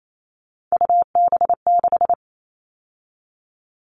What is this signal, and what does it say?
Morse code, heard as a keyed tone, about 28 wpm, 700 Hz. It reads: UB6
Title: All Cut Numbers - 3-Digit 28wpm